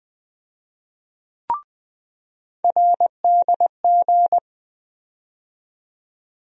Morse code, heard as a keyed tone, about 20 wpm, 700 Hz. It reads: RDG
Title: USA Call Sign Postfixes - 3 Letters 20wpm